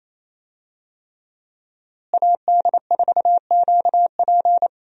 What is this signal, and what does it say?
Morse code, heard as a keyed tone, about 28 wpm, 700 Hz. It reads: AD4QP
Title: USA Call Signs - 5 Characters - Repeated 3x - 28wpm